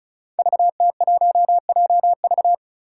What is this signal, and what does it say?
Morse code, heard as a keyed tone, about 36 wpm, 700 Hz. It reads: VT1JV
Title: Five Challenging Letter-Number Combinations 35wpm